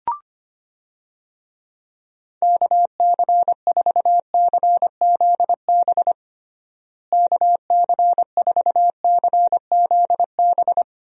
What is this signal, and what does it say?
Morse code, heard as a keyed tone, about 25 wpm, 700 Hz. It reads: KC4CZB KC4CZB
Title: USA Call Signs - 6 Characters - Repeated 2x - 25wpm